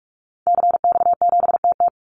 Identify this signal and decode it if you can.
Morse code, heard as a keyed tone, about 44 wpm, 700 Hz. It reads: CX7TT
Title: Call Signs Repeated 3x - 45wpm